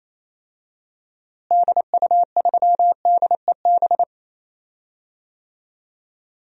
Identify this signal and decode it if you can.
Morse code, heard as a keyed tone, about 28 wpm, 700 Hz. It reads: DU3DEB
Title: DXCC-100 Call Signs - 6 Characters - Repeated 3x - 28wpm